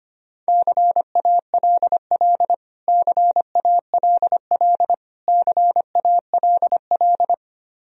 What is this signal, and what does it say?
Morse code, heard as a keyed tone, about 25 wpm, 700 Hz. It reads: CALL CALL CALL
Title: Top 100 Words Repeated 25wpm